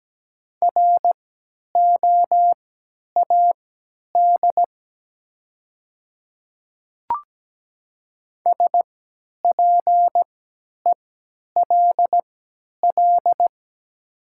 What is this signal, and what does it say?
Morse code, heard as a keyed tone, about 17 wpm, spaced as for 10 wpm, 700 Hz. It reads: ROAD SPELL
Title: 3x Character Spacing - Top 500 Words - 17wpm